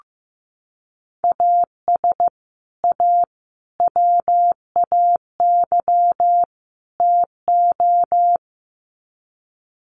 Morse code, heard as a keyed tone, about 15 wpm, 700 Hz. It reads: AS A WAY TO
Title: Sets of 4 Words 15wpm